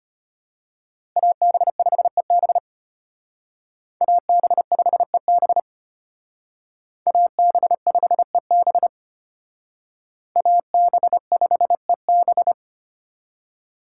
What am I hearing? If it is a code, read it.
Morse code, heard as a keyed tone, about 31 wpm, 700 Hz. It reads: AB5EB AB5EB AB5EB AB5EB
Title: Speed Racing - USA 2x2 Call Signs 25wpm